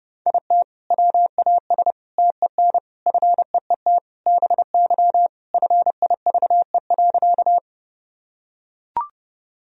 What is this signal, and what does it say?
Morse code, heard as a keyed tone, about 29 wpm, 700 Hz. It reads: IT WAS TEN FEET BY FIVE.
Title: Sentences from Top 1000 Words - Review Part 2 - 30wpm